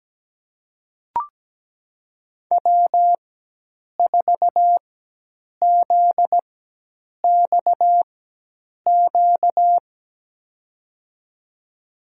Morse code, heard as a keyed tone, about 17 wpm, spaced as for 8 wpm, 700 Hz. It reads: W4ZXQ
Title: USA Call Signs - 5 Characters - 4x Character Spacing - 17wpm